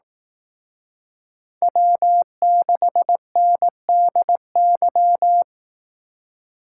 Morse code, heard as a keyed tone, about 18 wpm, 700 Hz. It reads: W6NDY
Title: Speed Racing - USA 1x3 Call Signs 15wpm